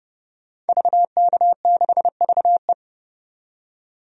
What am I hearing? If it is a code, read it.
Morse code, heard as a keyed tone, about 30 wpm, 700 Hz. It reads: VK6VE
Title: DXCC-100 Call Signs - 5 Characters - 30wpm